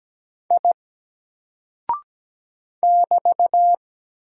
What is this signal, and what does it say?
Morse code, heard as a keyed tone, about 17 wpm, 700 Hz. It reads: I <BT>
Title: Single Character (Letter:Number:Punctuation:Prosign) 17wpm